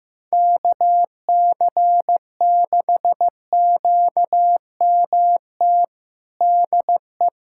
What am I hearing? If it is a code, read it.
Morse code, heard as a keyed tone, about 15 wpm, 700 Hz. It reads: KC6QMT DE
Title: USA Callsign DE Callsign 15wpm